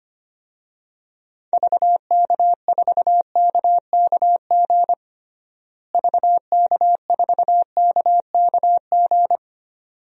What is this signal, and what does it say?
Morse code, heard as a keyed tone, about 25 wpm, 700 Hz. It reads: VK4KKG VK4KKG
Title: DXCC-100 Call Signs - 6 Characters - Repeated 2x - 25wpm